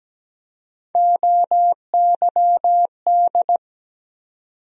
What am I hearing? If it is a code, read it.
Morse code, heard as a keyed tone, about 17 wpm, 700 Hz. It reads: OYD
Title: USA Call Sign Postfixes - 3 Letters 17wpm